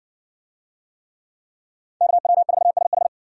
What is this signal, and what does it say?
Morse code, heard as a keyed tone, about 56 wpm, 700 Hz. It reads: KW4SH